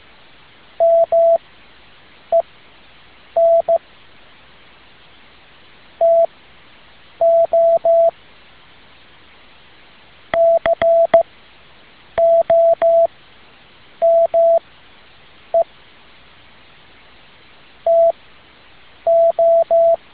Morse code, heard as a keyed tone, about 15 wpm, spaced as for 7 wpm, 670 Hz. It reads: MEN TO COME TO
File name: tin-upgrade / QSO 2013 02 12